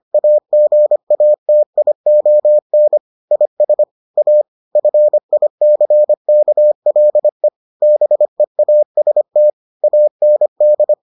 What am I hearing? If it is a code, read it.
Morse code, heard as a keyed tone, about 25 wpm, 600 Hz. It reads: AGATION IS A FICKLE BEAST AND